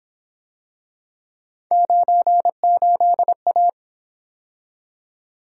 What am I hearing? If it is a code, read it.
Morse code, heard as a keyed tone, about 26 wpm, 700 Hz. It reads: 98A
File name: Speed-Racing - Three Letter-Number Combinations - 22wpm